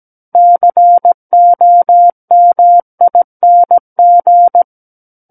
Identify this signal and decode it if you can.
Morse code, heard as a keyed tone, about 17 wpm, 700 Hz. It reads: COMING
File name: Book One / 08-Chapter 8 - 17 WPM